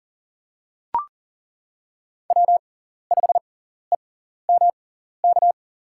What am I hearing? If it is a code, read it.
Morse code, heard as a keyed tone, about 38 wpm, spaced as for 14 wpm, 700 Hz. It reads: W5EMK
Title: USA Call Signs - 5 Characters - 6x Character Spacing - 40wpm